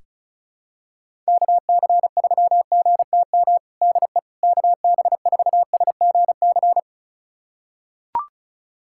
Morse code, heard as a keyed tone, about 36 wpm, 690 Hz. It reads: KC3GTM DE KB4SGC
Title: USA Callsign DE Callsign - 6 Characters - 35wpm